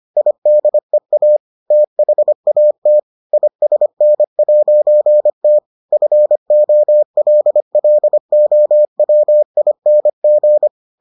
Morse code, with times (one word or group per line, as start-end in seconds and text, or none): 0.2-1.4 IDEA
1.7-3.0 THAT
3.3-5.6 ISN'T
5.9-10.7 FOLLOWING